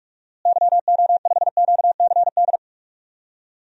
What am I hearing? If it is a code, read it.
Morse code, heard as a keyed tone, about 41 wpm, 700 Hz. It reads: YO5QCD
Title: DXCC-100 Call Signs - 6 Characters - Repeated 2x - 45wpm